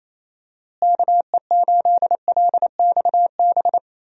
Morse code, heard as a keyed tone, about 28 wpm, 700 Hz. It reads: KE8LXB
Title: USA Call Signs - 6 Characters - Repeated 2x - 28wpm